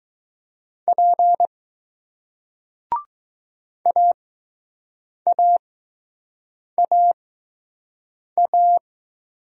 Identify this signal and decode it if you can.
Morse code, heard as a keyed tone, about 20 wpm, spaced as for 6 wpm, 700 Hz. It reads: P AAAA